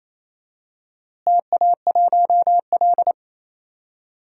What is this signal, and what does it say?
Morse code, heard as a keyed tone, about 28 wpm, 700 Hz. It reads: TA1L
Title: DXCC-100 Call Signs - 4 Characters - Repeated 2x - 28wpm